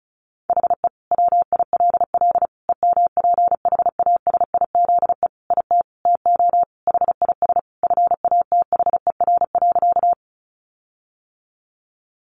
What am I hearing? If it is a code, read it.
Morse code, heard as a keyed tone, about 34 wpm, 700 Hz. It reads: HE WILL EMPHASIZE IT TO HIS FATHER.
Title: Sentences from Top 1500 Words 35wpm